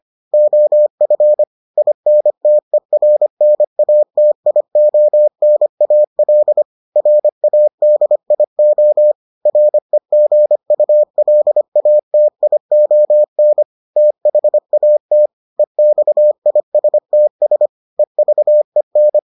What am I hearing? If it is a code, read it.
Morse code, heard as a keyed tone, about 25 wpm, 600 Hz. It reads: OF INTERNATIONAL RADIO REGULATION THAT EXISTS EVEN